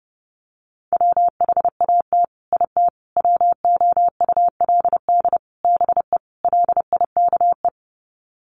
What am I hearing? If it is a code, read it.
Morse code, heard as a keyed tone, about 30 wpm, 700 Hz. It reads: WHAT IT WOULD BE LIKE
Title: Sets of 5 Words Encore 30wpm